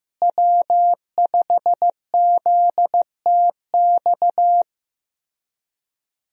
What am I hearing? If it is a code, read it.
Morse code, heard as a keyed tone, about 15 wpm, 700 Hz. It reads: W5ZTX
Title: USA Call Signs - 5 Characters - Repeated 2x - 15wpm